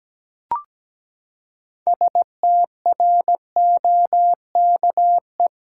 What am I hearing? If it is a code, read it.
Morse code, heard as a keyed tone, about 17 wpm, 700 Hz. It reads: STROKE
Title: Six Letter Words - 17wpm